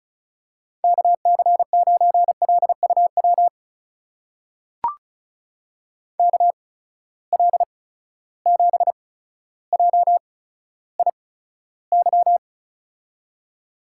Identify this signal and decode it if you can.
Morse code, heard as a keyed tone, about 35 wpm, spaced as for 15 wpm, 700 Hz. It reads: KC9LUW KL7JIY
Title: USA Call Signs - 6 Characters - 8x Character Spacing - 35wpm